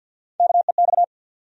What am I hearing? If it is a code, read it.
Morse code, heard as a keyed tone, about 46 wpm, 700 Hz. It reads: KEX
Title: USA Call Sign Postfixes - 3 Letters 50wpm